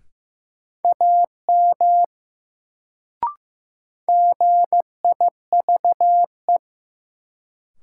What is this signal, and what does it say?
Morse code, heard as a keyed tone, about 15 wpm, 700 Hz. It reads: AM GIVE
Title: Sets of 2 Words - New Word Drill - 15wpm